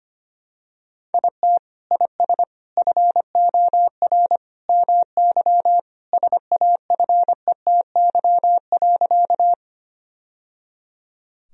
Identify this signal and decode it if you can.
Morse code, heard as a keyed tone, about 25 wpm, 700 Hz. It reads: IT IS FOR MY SAFETY.
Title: Sentences from Top 1000 Words - Review Part 4 - 25wpm